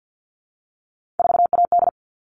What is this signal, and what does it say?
Morse code, heard as a keyed tone, about 50 wpm, 700 Hz. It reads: 4AD
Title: All Cut Numbers - 3-Digit 50wpm